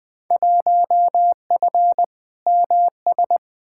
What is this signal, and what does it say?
Morse code, heard as a keyed tone, about 20 wpm, 700 Hz. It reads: 1F MS
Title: Field Day Exchanges Encore - 20wpm